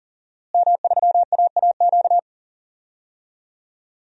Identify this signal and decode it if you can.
Morse code, heard as a keyed tone, about 39 wpm, 700 Hz. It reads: M3AAQ
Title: DXCC-100 Call Signs - 5 Characters - Repeated 3x - 40wpm